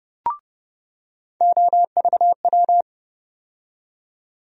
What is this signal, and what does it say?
Morse code, heard as a keyed tone, about 30 wpm, 700 Hz. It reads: OVW
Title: USA Call Sign Postfixes - 3 Letters 30wpm